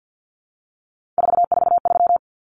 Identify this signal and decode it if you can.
Morse code, heard as a keyed tone, about 50 wpm, 700 Hz. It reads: 443
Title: Common Cut Numbers - 3-Digit 50wpm